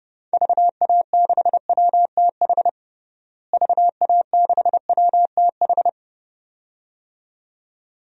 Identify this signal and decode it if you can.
Morse code, heard as a keyed tone, about 30 wpm, 700 Hz. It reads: VA6WTH VA6WTH
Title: DXCC-100 Call Signs - 6 Characters - Repeated 2x - 30wpm